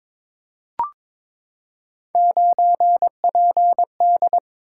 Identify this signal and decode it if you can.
Morse code, heard as a keyed tone, about 22 wpm, 700 Hz. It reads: 9PD
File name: practice / 22wpm / Three Letter-Number Combinations - 22wpm